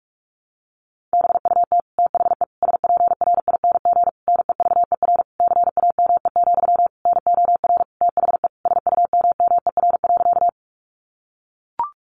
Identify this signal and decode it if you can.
Morse code, heard as a keyed tone, about 44 wpm, 700 Hz. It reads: BUT THE SPRING NEVER CAME, NOR THE SUMMER.
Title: Sentences from Top 1500 Words 45wpm